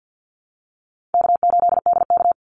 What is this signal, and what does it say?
Morse code, heard as a keyed tone, about 54 wpm, 700 Hz. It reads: K8DK
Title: Easy Call Signs Repeated 2x - 50wpm